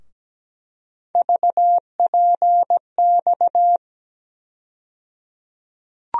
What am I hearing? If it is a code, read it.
Morse code, heard as a keyed tone, about 17 wpm, 700 Hz. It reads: VPX
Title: USA Call Sign Postfixes - 3 Letters 17wpm